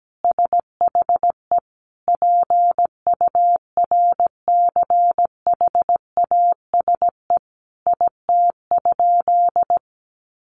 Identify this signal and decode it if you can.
Morse code, heard as a keyed tone, about 17 wpm, 700 Hz. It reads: SHE PURCHASE IT?
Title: Sentences from Top 800 Words 17wpm